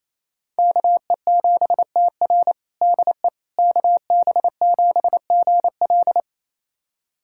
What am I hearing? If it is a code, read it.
Morse code, heard as a keyed tone, about 28 wpm, 700 Hz. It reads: KE7TR DE KB7GL
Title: USA Callsign DE Callsign - 5 Characters - 28wpm